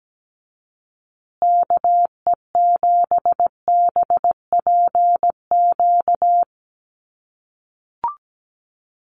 Morse code, heard as a keyed tone, about 17 wpm, 700 Hz. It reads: KE7BPQ
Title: USA Call Signs Repeated 2x 17wpm